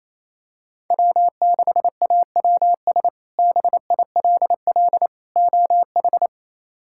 Morse code, heard as a keyed tone, about 28 wpm, 700 Hz. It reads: W6AWS BILL OH
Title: K1USN SST QSO Practice - 28wpm